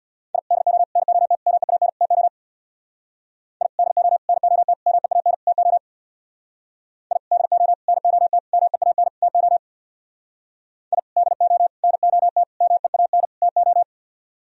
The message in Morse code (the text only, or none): I DO NOT MEAN TO I DO NOT MEAN TO I DO NOT MEAN TO I DO NOT MEAN TO